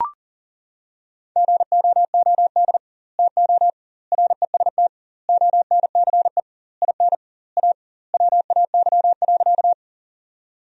Morse code, heard as a keyed tone, about 40 wpm, 700 Hz. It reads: GOOD TO REST ONCE IN A WAY.
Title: 2x Word-Spacing - Sentences from Top 500 Words - 40wpm